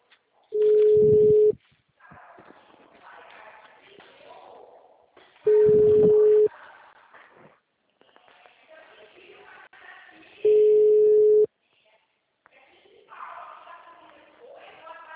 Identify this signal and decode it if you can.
Ringback tone (Europe, 425 Hz), with speech.
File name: out- -2016- - - .18303